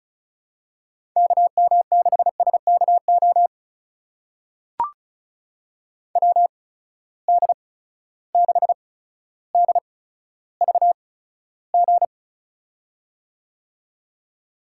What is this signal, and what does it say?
Morse code, heard as a keyed tone, about 36 wpm, spaced as for 15 wpm, 700 Hz. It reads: KM6SKO WD6DVG
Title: USA Call Signs - 6 Characters - 8x Character Spacing - 35wpm